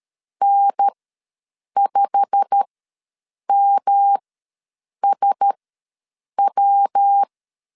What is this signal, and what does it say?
Morse code, heard as a keyed tone, about 13 wpm, spaced as for 7 wpm, 800 Hz. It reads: N5MSW
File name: LEZIONI / 8 YJ45 / 8 YJ45 ws 12wpm 3146